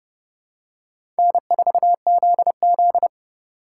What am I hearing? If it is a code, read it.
Morse code, heard as a keyed tone, about 30 wpm, 700 Hz. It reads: N4ZZ
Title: USA Call Signs - 4 Characters - 4x Character Spacing - 30wpm